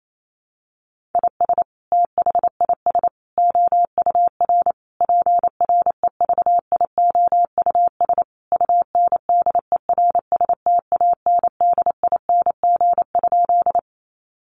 Morse code, heard as a keyed tone, about 28 wpm, 700 Hz. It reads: IS THIS OUR PREVIOUS UNDERSTANDING?
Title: Sentences from Top 1500 Words 28wpm